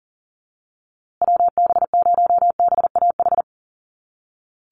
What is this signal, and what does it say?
Morse code, heard as a keyed tone, about 40 wpm, 700 Hz. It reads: WB0BAH
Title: USA Call Signs Repeated 3x 40wpm